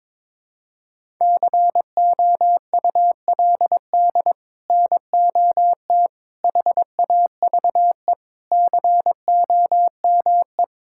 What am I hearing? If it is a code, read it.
Morse code, heard as a keyed tone, about 21 wpm, 700 Hz. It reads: COULD NOT HAVE COME
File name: practice / 22wpm / Sets of 8 Words - 22wpm